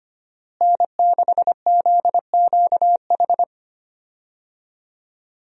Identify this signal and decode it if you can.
Morse code, heard as a keyed tone, about 25 wpm, 700 Hz. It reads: N6ZQH
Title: USA Call Signs - 5 Characters - Repeated 2x - 25wpm